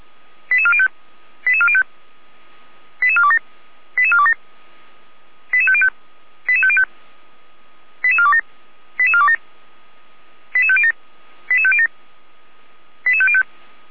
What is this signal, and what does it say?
Five-tone selective call (ZVEI), heard as quick runs of five tones, tones of about 70 ms. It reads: 88405 88405 88427 88427 88505 88505 88427 88427 88507 88507 88505
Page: /455/SDR/ZVEI/88447/2021.08.16/